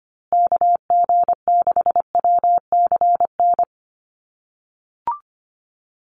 Morse code, heard as a keyed tone, about 25 wpm, 700 Hz. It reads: KG6WCN
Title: USA Call Signs Repeated 3x 25wpm